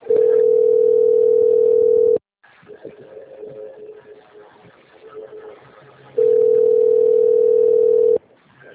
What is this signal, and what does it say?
Ringback tone (North America), 440 + 480 Hz.
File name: out- - - - - .26479